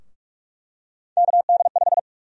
Morse code, heard as a keyed tone, about 48 wpm, 700 Hz. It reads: KD5
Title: USA Call Sign Prefixes - 45wpm